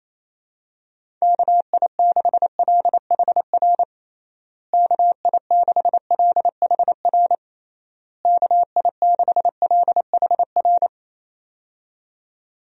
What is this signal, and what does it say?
Morse code, heard as a keyed tone, about 28 wpm, 700 Hz. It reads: KI6LHR KI6LHR KI6LHR
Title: USA Call Signs - 6 Characters - Repeated 3x - 28wpm